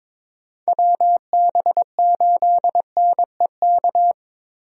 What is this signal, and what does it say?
Morse code, heard as a keyed tone, about 22 wpm, 700 Hz. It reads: WB8NEK
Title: USA Call Signs - 6 Characters - Repeated 2x - 22wpm